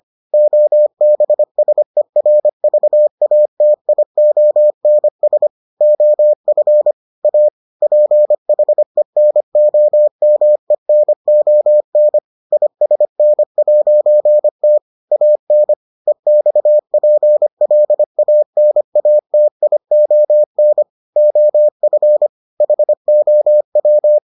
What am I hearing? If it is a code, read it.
Morse code, heard as a keyed tone, about 25 wpm, 600 Hz. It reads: OBSERVATIONS OF A PHENOMENON ISN'T AN EXPLANATION OF HOW